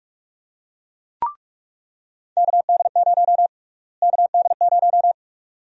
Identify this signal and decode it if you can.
Morse code, heard as a keyed tone, about 43 wpm, 700 Hz. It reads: KD0 KD0
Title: USA Call Sign Prefixes - Repeated 2x - 45wpm